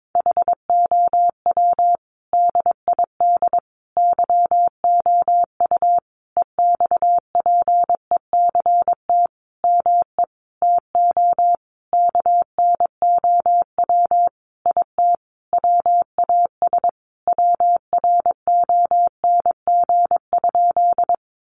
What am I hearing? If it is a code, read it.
Morse code, heard as a keyed tone, about 22 wpm, 700 Hz. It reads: HOW DID YOU EXPECT ME TO KNOW IT WAS WRONG?
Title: Sentences from Top 1100 Words 22wpm